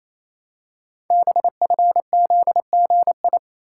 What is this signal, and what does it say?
Morse code, heard as a keyed tone, about 28 wpm, 700 Hz. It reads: BFZGI
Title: Code Groups - 5 Letters - 28wpm